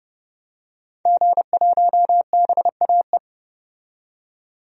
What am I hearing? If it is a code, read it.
Morse code, heard as a keyed tone, about 31 wpm, 700 Hz. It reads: G1BAE
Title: DXCC-100 Call Signs - 5 Characters - 30wpm